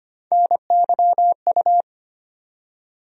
Morse code, heard as a keyed tone, about 25 wpm, 700 Hz. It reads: NYU
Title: USA Call Sign Postfixes - 3 Letters 25wpm